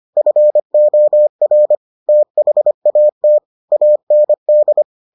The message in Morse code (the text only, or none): FOR THAT AND